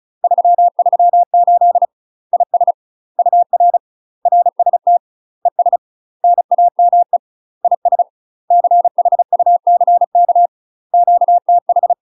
Morse code, heard as a keyed tone, about 34 wpm, 700 Hz. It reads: 338 IS UR RST ES NAME IS CHUCK QTH
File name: UK MASTER QSOs TRAINING/35WPM